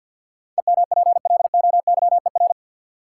Morse code, heard as a keyed tone, about 50 wpm, 700 Hz. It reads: EMPLOYER
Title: Sentences from Top 1600 Words - New Word Drill - 50wpm